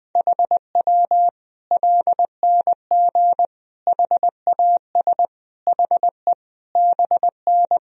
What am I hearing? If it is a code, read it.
Morse code, heard as a keyed tone, about 20 wpm, 700 Hz. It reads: HW LNG HAS HE BN
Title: Sentences from Top 100 Words and Abbreviations 20wpm